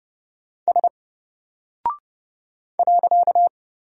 Morse code, heard as a keyed tone, about 29 wpm, 700 Hz. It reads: S .
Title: Single Character (Letter:Number:Punctuation:Prosign) 30wpm